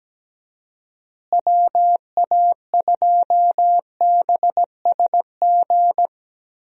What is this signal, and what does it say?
Morse code, heard as a keyed tone, about 17 wpm, 700 Hz. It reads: WA2BSG
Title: USA Call Signs - 6 Characters - Repeated 2x - 17wpm